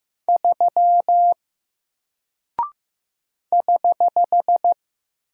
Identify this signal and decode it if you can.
Morse code, heard as a keyed tone, about 15 wpm, 700 Hz. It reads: 3 <HH>
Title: Single Character (Letter:Number:Punctuation:Prosign) Alt 15wpm